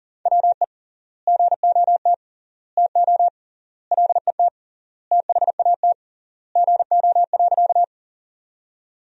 Morse code, heard as a keyed tone, about 40 wpm, 700 Hz. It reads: WE GOT TO LET THAT GO.
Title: 3x Word-Spacing - Sentences from Top 300 Words - 40wpm